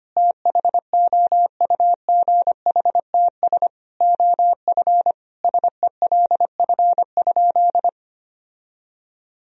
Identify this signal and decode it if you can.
Morse code, heard as a keyed tone, about 25 wpm, 700 Hz. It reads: THOUGHTS OF SELF?
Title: Sentences from Top 1000 Words - Review Part 1 - 25wpm